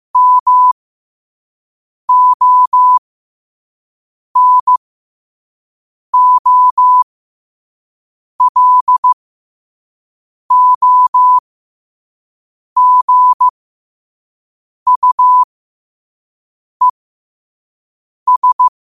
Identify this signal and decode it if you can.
Morse code, heard as a keyed tone, about 15 wpm, spaced as for 5 wpm, 1000 Hz. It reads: MONOLOGUES
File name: qotdfiles / 5wpm